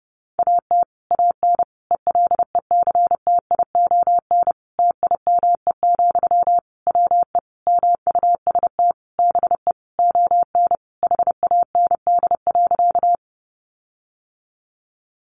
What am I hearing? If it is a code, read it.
Morse code, heard as a keyed tone, about 28 wpm, 700 Hz. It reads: AT AN ELECTION TIME, WE MUST BE ON HAND.
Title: Sentences from Top 1500 Words 30wpm